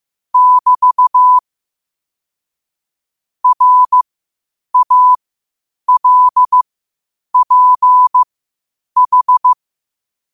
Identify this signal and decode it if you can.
Morse code, heard as a keyed tone, about 15 wpm, spaced as for 8 wpm, 1000 Hz. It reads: <BT> RALPH
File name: qotdfiles / 8wpm